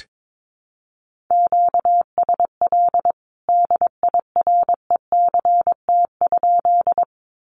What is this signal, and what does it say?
Morse code, heard as a keyed tone, about 22 wpm, 700 Hz. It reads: QSL DIRECT?
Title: Common QSO Phrases 22wpm